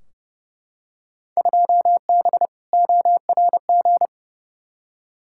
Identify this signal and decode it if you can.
Morse code, heard as a keyed tone, about 30 wpm, 700 Hz. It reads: 2B ORG
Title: Field Day Exchanges Encore - 30wpm